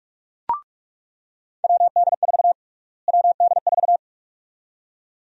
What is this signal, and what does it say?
Morse code, heard as a keyed tone, about 42 wpm, 700 Hz. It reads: WD4 WD4
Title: USA Call Sign Prefixes - Repeated 2x - 45wpm